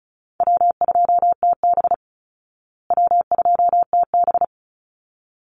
Morse code, heard as a keyed tone, about 34 wpm, 700 Hz. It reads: W2TB W2TB
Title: Call Signs Repeated 3x - 35wpm